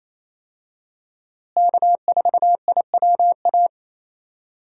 Morse code, heard as a keyed tone, about 28 wpm, 700 Hz. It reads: K4IWA
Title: USA Call Signs - 5 Characters - Repeated 3x - 28wpm